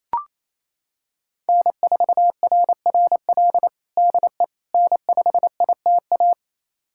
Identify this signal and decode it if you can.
Morse code, heard as a keyed tone, about 28 wpm, 700 Hz. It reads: N4RRL DE N5ITA
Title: USA Callsign DE Callsign - 5 Characters - 28wpm